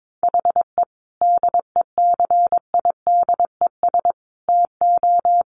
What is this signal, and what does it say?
Morse code, heard as a keyed tone, about 22 wpm, 700 Hz. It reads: HE DECIDES TO
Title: Additional Easy Phrases 22wpm